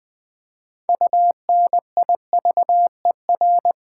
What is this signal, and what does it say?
Morse code, heard as a keyed tone, about 20 wpm, 700 Hz. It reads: UNIVER
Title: Build A Word - 2 Letters At A Time - 8-Letter Words - 20wpm